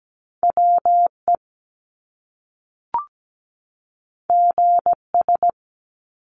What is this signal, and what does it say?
Morse code, heard as a keyed tone, about 17 wpm, 700 Hz. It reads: WE GS
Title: Top Two Letter Combinations in Words 17wpm